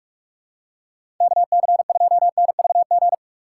Morse code, heard as a keyed tone, about 46 wpm, 700 Hz. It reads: KC2NVG
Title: USA Call Signs - 6x Character Spacing - 45wpm